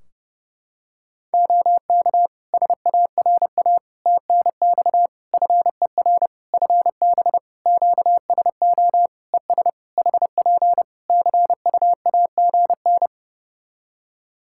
Morse code, heard as a keyed tone, about 28 wpm, 700 Hz. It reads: OK SARA TNX FER FB QSO ES HP CUAGN
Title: Practice QSO - Exchange 4 - Ending - 30wpm